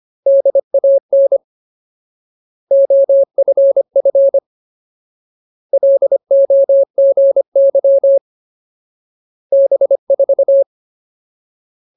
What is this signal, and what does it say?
Morse code, heard as a keyed tone, about 25 wpm, 550 Hz. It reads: DAN OFF LOGY B4
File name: Ultimate Word List / 25WPM Ultimate